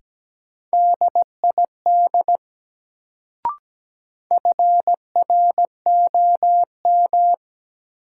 Morse code, heard as a keyed tone, about 17 wpm, 700 Hz. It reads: DID FROM
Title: Sentences from Top 100 Words - New Word Drill - 17wpm